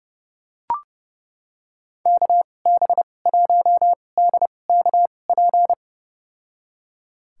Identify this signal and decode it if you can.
Morse code, heard as a keyed tone, about 30 wpm, spaced as for 22 wpm, 700 Hz. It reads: KB1DKP
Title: USA Call Signs - 6 Characters - 2x Character Spacing - 30wpm